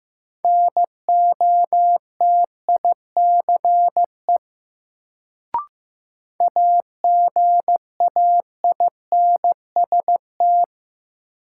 Morse code, heard as a keyed tone, about 15 wpm, 700 Hz. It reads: NOTICE AGAINST
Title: Sentences from Top 400 Words - New Word Drill - 15wpm